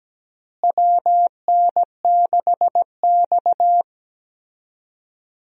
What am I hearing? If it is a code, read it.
Morse code, heard as a keyed tone, about 17 wpm, 700 Hz. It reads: WN6X
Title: USA Call Signs - 4 Characters - Repeated 3x - 17wpm